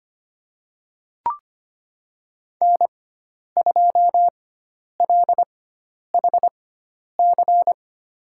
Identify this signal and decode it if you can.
Morse code, heard as a keyed tone, about 25 wpm, spaced as for 10 wpm, 700 Hz. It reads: N2LHC
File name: Lesson 16g - Letter c - Call Signs 25wpm